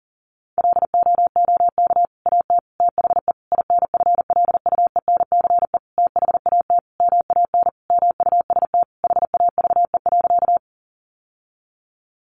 Morse code, heard as a keyed tone, about 40 wpm, 700 Hz. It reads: LOOK AT THE INFLUENCE THAT MAN MUST HAVE.
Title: Sentences from Top 1100 Words 40wpm